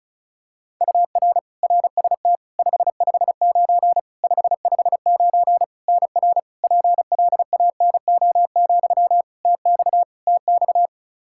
Morse code, heard as a keyed tone, about 34 wpm, 700 Hz. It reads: UR RST 559 559 NR PLANO, TX TX
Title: Practice QSO - Exchange 1 - Beginning - 35wpm